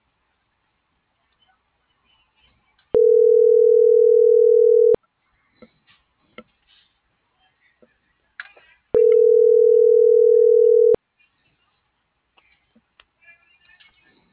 Ringback tone (North America), 440 + 480 Hz.